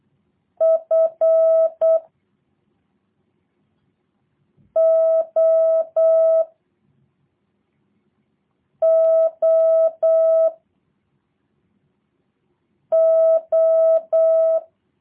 Morse code, heard as a keyed tone, about 8 wpm, 650 Hz. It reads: F O O O
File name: jh8cbh / 2023cw / review1-4